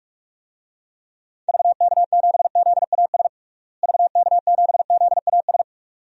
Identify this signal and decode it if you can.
Morse code, heard as a keyed tone, about 43 wpm, 700 Hz. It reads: VK7ZAS VK7ZAS
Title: DXCC-100 Call Signs - 6 Characters - Repeated 2x - 45wpm